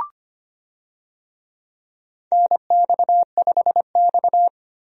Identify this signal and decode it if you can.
Morse code, heard as a keyed tone, about 25 wpm, 700 Hz. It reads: NX5X
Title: USA Call Signs - 4 Characters - 25wpm